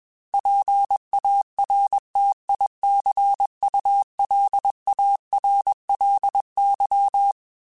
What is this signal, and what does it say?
Morse code, heard as a keyed tone, about 21 wpm, 800 Hz. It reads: PARTICULARLY
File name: Technical Subjects / Technical Subjects-mp3s / Reactance @20